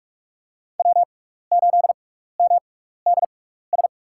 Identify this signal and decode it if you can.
Morse code, heard as a keyed tone, about 47 wpm, spaced as for 16 wpm, 690 Hz. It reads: W8MDS
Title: USA Call Signs - 5 Characters - 6x Character Spacing - 45wpm